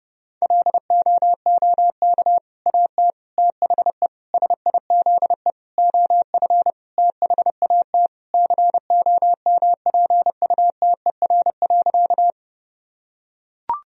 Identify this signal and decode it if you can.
Morse code, heard as a keyed tone, about 30 wpm, 700 Hz. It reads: LOOK AT THE SIZE OF THAT COMPUTER.
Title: Sentences from Top 1000 Words - Review Part 3 - 30wpm